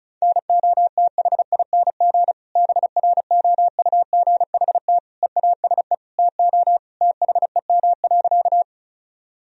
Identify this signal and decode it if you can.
Morse code, heard as a keyed tone, about 34 wpm, 700 Hz. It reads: NOTHING BROUGHT EASE TO THEM.
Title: Sentences from Top 1000 Words - Review Part 1 - 35wpm